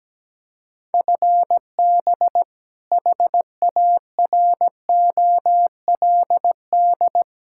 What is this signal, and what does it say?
Morse code, heard as a keyed tone, about 17 wpm, 700 Hz. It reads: FB HAROLD
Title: Practice QSO - Exchange 3 - More Info - 17wpm